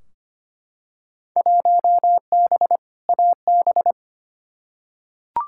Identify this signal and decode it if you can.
Morse code, heard as a keyed tone, about 25 wpm, 700 Hz. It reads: 1B AB
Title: Field Day Exchanges - 25wpm